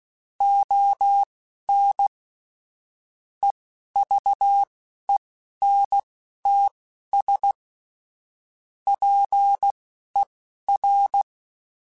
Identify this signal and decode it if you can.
Morse code, heard as a keyed tone, about 16 wpm, spaced as for 11 wpm, 800 Hz. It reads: ON EVENTS PER